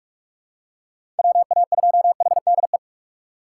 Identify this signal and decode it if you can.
Morse code, heard as a keyed tone, about 45 wpm, 700 Hz. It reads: WA2HDE
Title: USA Call Signs - 6 Characters - 6x Character Spacing - 45wpm